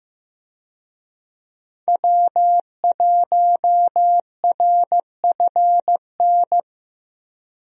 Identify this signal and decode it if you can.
Morse code, heard as a keyed tone, about 15 wpm, 700 Hz. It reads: W1RFN
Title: USA Call Signs - 5 Characters - Repeated 3x - 15wpm